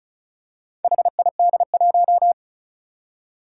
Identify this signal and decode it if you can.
Morse code, heard as a keyed tone, about 34 wpm, 700 Hz. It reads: HID1
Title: Four Challenging Letter-Number Combinations 35wpm